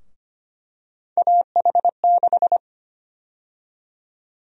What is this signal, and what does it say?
Morse code, heard as a keyed tone, about 25 wpm, 700 Hz. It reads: AH6
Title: USA Call Sign Prefixes - 25wpm